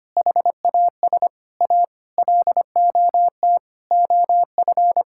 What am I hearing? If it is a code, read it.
Morse code, heard as a keyed tone, about 25 wpm, 700 Hz. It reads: HAS A LOT OF
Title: Sentences from Top 1000 Words - Review Part 6 - 25wpm